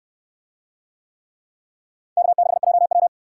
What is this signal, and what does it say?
Morse code, heard as a keyed tone, about 62 wpm, 700 Hz. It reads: K6JW